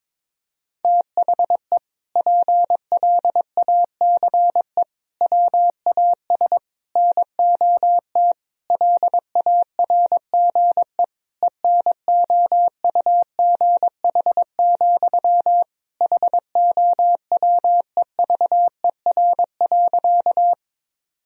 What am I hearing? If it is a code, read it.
Morse code, heard as a keyed tone, about 22 wpm, 700 Hz. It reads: THE PLACE WAS NOT LARGE ENOUGH, HOWEVER.
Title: Sentences from Top 1000 Words - Review Part 1 - 22wpm